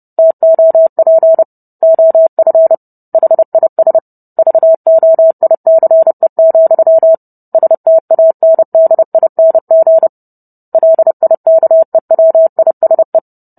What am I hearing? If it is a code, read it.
Morse code, heard as a keyed tone, about 30 wpm, 650 Hz. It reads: TOP OF HIS VOICE, STANDING LIKEWISE